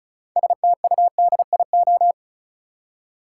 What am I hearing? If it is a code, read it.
Morse code, heard as a keyed tone, about 36 wpm, 700 Hz. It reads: STUDIO
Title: Six Letter Words - 35wpm